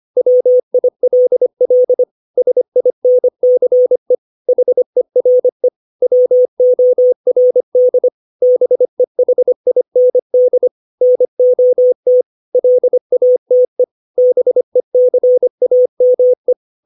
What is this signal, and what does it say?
Morse code, heard as a keyed tone, about 25 wpm, 500 Hz. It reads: WILL SINCE HERE WORD BEHIND NOT LATE BECAME
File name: 632 words / 632 Words 25WPM